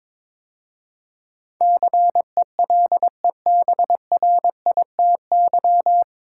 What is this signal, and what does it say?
Morse code, heard as a keyed tone, about 22 wpm, 700 Hz. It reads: CELEBRITY
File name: 9 Letter Words - First 3 Letters Given - 22wpm